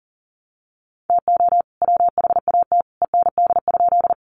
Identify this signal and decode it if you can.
Morse code, heard as a keyed tone, about 38 wpm, 700 Hz. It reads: TO WHAT END?
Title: Sentences from Top 200 Words 40wpm